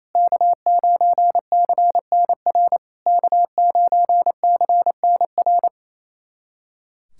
Morse code, heard as a keyed tone, about 28 wpm, 690 Hz. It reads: K9CNR K9CNR
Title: SOTA QSO Practice 28wpm